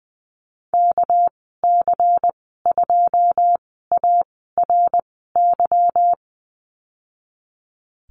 Morse code, heard as a keyed tone, about 20 wpm, spaced as for 15 wpm, 700 Hz. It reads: KC2ARY
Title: 200 Lesson 32g - Letter k - Call Signs 20wpm